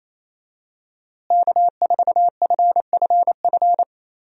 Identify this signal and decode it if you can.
Morse code, heard as a keyed tone, about 27 wpm, 700 Hz. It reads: K4FFF
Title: USA Call Signs - 5 Characters - Repeated 3x - 28wpm